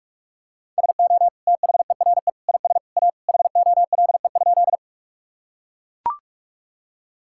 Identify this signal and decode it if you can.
Morse code, heard as a keyed tone, about 41 wpm, 700 Hz. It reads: SO THERE IS A HOLE?
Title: Sentences from Top 1600 Words 45wpm